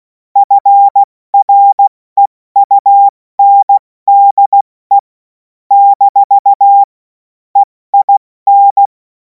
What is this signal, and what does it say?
Morse code, heard as a keyed tone, about 16 wpm, 800 Hz. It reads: FREUNDE - EIN
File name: Morsekurs / Witze / Witz 24. Februar 2026 70 F80